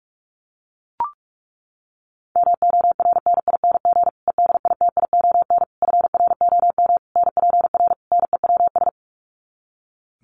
Morse code, heard as a keyed tone, about 44 wpm, 700 Hz. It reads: MORNING EDITION FROM NPR NEWS
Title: Sets of 5 Words Encore 45wpm